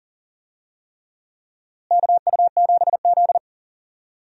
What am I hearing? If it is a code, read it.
Morse code, heard as a keyed tone, about 40 wpm, 700 Hz. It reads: KU7Z
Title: USA Call Signs - 4 Characters - Repeated 3x - 40wpm